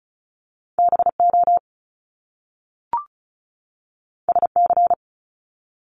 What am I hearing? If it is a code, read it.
Morse code, heard as a keyed tone, about 36 wpm, 700 Hz. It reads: B O S C
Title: Top Two Letter Combinations in Words 35wpm